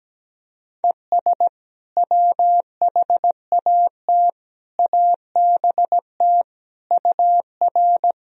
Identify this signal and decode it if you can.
Morse code, heard as a keyed tone, about 17 wpm, 700 Hz. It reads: ES WHAT ABT UR
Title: Sentences from Top 200 Words and Abbreviations 17wpm